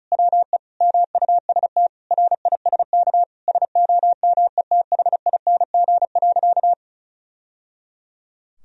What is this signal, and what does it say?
Morse code, heard as a keyed tone, about 36 wpm, 700 Hz. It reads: WE MUST RISK SOMETHING.
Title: Sentences from Top 1000 Words - Review Part 4 - 35wpm